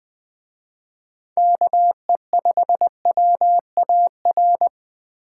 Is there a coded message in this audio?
Yes: Morse code from 1.4 to 4.7 s, heard as a keyed tone, about 20 wpm, 700 Hz. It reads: KE5WAR